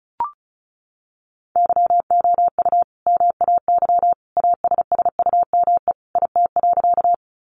Morse code, heard as a keyed tone, about 34 wpm, 700 Hz. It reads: YOU MAY ASSUME IT.
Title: Sentences from Top 1500 Words 35wpm